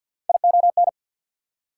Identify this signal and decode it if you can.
Morse code, heard as a keyed tone, about 47 wpm, 690 Hz. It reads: ION
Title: Prefixes and Suffixes 50wpm